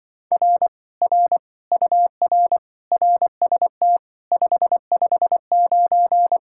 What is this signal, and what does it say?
Morse code, heard as a keyed tone, about 25 wpm, 700 Hz. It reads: R R UR RST 559